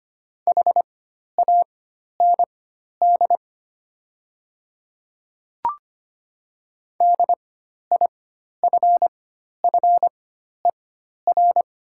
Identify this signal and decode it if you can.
Morse code, heard as a keyed tone, about 25 wpm, spaced as for 12 wpm, 700 Hz. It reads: HAND DIFFER
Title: 4x Character Spacing - Top 200 Words - 25wpm